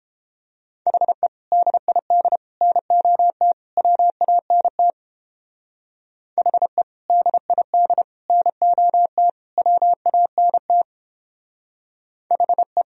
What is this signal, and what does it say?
Morse code, heard as a keyed tone, about 29 wpm, 700 Hz. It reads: HE DID NOT WANT HE DID NOT WANT HE